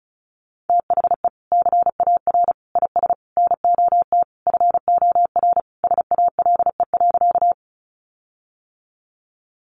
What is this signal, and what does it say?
Morse code, heard as a keyed tone, about 34 wpm, 700 Hz. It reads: THE CAR IS NOT FOR SALE.
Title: Sentences from Top 1400 Words 35wpm